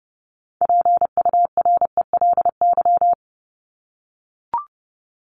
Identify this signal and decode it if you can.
Morse code, heard as a keyed tone, about 30 wpm, 700 Hz. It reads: PURELY
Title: LY Suffix - Top 100 Words with Base Words 30wpm